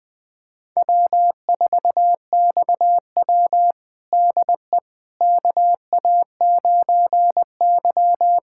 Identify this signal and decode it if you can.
Morse code, heard as a keyed tone, about 21 wpm, 700 Hz. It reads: W4XW DE KA9Y
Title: USA Callsign DE Callsign - 4 Characters - 20wpm